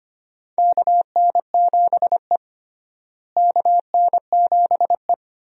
Morse code, heard as a keyed tone, about 25 wpm, 700 Hz. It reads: KN7E KN7E
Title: USA Call Signs - 4 Characters - Repeated 3x - 25wpm